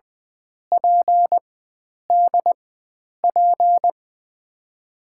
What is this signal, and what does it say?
Morse code, heard as a keyed tone, about 20 wpm, spaced as for 9 wpm, 700 Hz. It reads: PDP
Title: USA Call Signs - 5 Characters - 4x Character Spacing - 20wpm